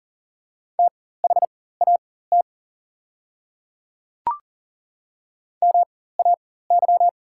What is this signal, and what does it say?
Morse code, heard as a keyed tone, about 40 wpm, spaced as for 19 wpm, 700 Hz. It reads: THAT MAY
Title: 4x Character Spacing - Top 100 Words - 40wpm